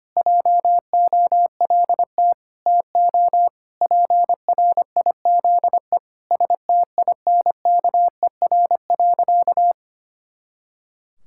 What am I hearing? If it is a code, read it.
Morse code, heard as a keyed tone, about 25 wpm, 700 Hz. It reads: JOLT TO PRIZE STINKER.
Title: Pangram Challenge v2 - 25wpm